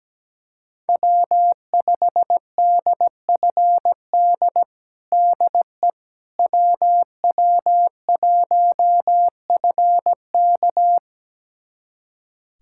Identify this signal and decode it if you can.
Morse code, heard as a keyed tone, about 17 wpm, 700 Hz. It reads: W5DFD DE WW1FK
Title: USA Callsign DE Callsign - 5 Characters - 17wpm